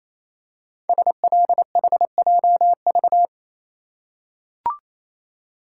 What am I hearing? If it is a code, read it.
Morse code, heard as a keyed tone, about 29 wpm, 690 Hz. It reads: SLHJV
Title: Code Groups - 5 Letters - 28wpm